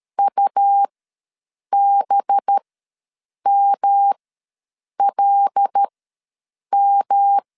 Morse code, heard as a keyed tone, about 13 wpm, spaced as for 7 wpm, 800 Hz. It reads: UBMLM
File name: 5 LB ws 12wpm